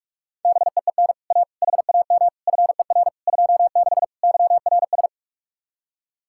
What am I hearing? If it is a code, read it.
Morse code, heard as a keyed tone, about 58 wpm, 700 Hz. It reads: BEEN A HAM FER 26 YRS